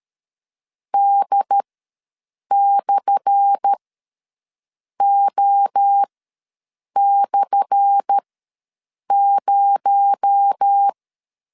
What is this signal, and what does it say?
Morse code, heard as a keyed tone, about 13 wpm, spaced as for 6 wpm, 800 Hz. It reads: D/O/0
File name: LEZIONI / 10 SIGNS / 10A SLASH ws 12wpm 3443